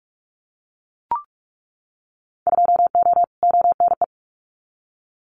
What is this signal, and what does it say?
Morse code, heard as a keyed tone, about 46 wpm, 690 Hz. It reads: 2O ONE
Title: Winter Field Day Exchanges Encore 45wpm